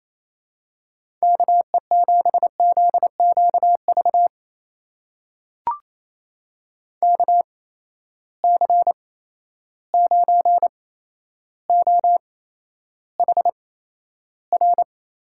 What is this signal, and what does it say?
Morse code, heard as a keyed tone, about 28 wpm, spaced as for 12 wpm, 700 Hz. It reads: KE7ZQV KC9OHR
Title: USA Call Signs - 6 Characters - 8x Character Spacing - 28wpm